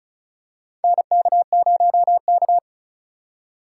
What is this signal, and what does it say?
Morse code, heard as a keyed tone, about 36 wpm, 700 Hz. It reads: NK0K
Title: USA Call Signs - 4x Character Spacing - 35wpm